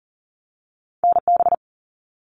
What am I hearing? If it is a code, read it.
Morse code, heard as a keyed tone, about 38 wpm, 700 Hz. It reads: NB
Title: Canadian Provinces and Territories 40wpm